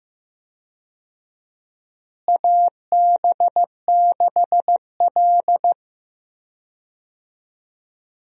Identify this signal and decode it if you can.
Morse code, heard as a keyed tone, about 15 wpm, 700 Hz. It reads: AB6L
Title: USA Call Signs - 4 Characters - Repeated 3x - 15wpm